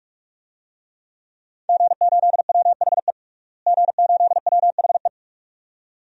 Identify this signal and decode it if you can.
Morse code, heard as a keyed tone, about 46 wpm, 700 Hz. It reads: G8WHE G8WHE
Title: DXCC-100 Call Signs - 5 Characters - Repeated 2x - 45wpm